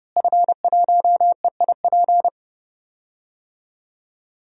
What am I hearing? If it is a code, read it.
Morse code, heard as a keyed tone, about 30 wpm, 700 Hz. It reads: F1EIP
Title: DXCC-100 Call Signs - 5 Characters - Repeated 3x - 30wpm